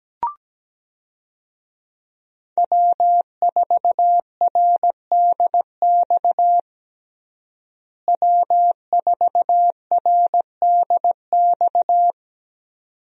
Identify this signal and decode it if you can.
Morse code, heard as a keyed tone, about 17 wpm, 700 Hz. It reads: W4RDX W4RDX
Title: USA Call Signs - 5 Characters - Repeated 2x - 17wpm